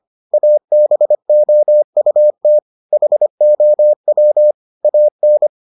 Morse code, heard as a keyed tone, about 25 wpm, 600 Hz. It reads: ABOUT HOW AN